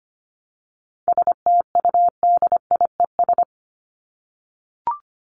Morse code, heard as a keyed tone, about 25 wpm, 700 Hz. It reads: STUDIES
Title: ES:S Suffix - Top 100 Words with Base Words 25wpm